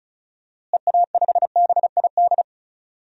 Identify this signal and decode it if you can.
Morse code, heard as a keyed tone, about 36 wpm, 700 Hz. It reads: EA5BID
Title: DXCC-100 Call Signs - 6 Characters - Repeated 2x - 35wpm